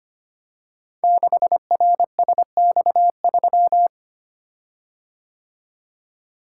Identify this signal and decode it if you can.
Morse code, heard as a keyed tone, about 25 wpm, 700 Hz. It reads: 6RSX3
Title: Code Groups - 5 Letter:Numbers - 25wpm